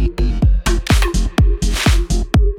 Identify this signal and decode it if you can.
There is no code.